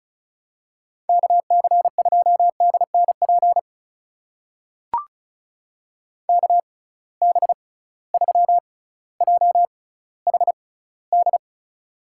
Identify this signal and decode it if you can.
Morse code, heard as a keyed tone, about 36 wpm, spaced as for 18 wpm, 700 Hz. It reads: KC2DNP KB3JHD
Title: USA Call Signs - 6 Characters - 6x Character Spacing - 35wpm